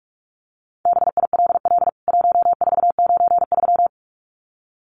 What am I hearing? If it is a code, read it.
Morse code, heard as a keyed tone, about 44 wpm, 700 Hz. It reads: BILL 1493
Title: CWT Exchange 45wpm